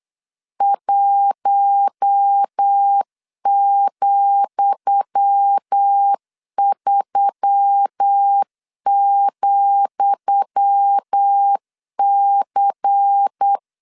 Morse code, heard as a keyed tone, about 8 wpm, 800 Hz. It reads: 1,3,C
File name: LEZIONI / 10 SIGNS / 10D COMA 8wpm 3749